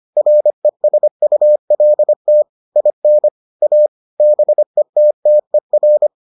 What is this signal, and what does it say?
Morse code, heard as a keyed tone, about 25 wpm, 600 Hz. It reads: RESULT IN A BETTER